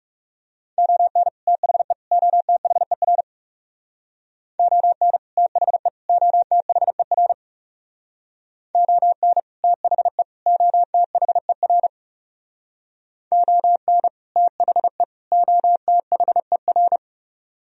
Morse code, heard as a keyed tone, about 37 wpm, 700 Hz. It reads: ON THE OTHER ON THE OTHER ON THE OTHER ON THE OTHER
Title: Speed-Racing - Sets of 3 Words 30wpm